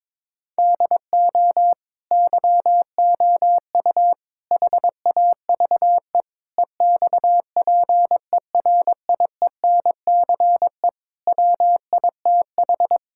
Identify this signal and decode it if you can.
Morse code, heard as a keyed tone, about 22 wpm, 700 Hz. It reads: DO YOU HAVE EXPERIENCE WITH
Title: Sentences from Top 1000 Words - Review Part 5 - 22wpm